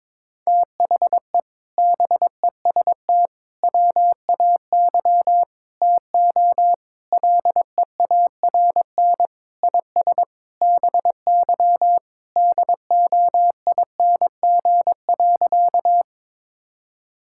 Morse code, heard as a keyed tone, about 22 wpm, 700 Hz. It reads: THE BEST WAY TO LEARN IS BY DOING.
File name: practice / 22wpm / Sentences with 4-Word Phrases - 22wpm